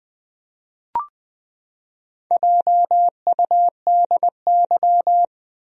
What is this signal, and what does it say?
Morse code, heard as a keyed tone, about 20 wpm, 700 Hz. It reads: JUDY
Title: USA Common Names - 20wpm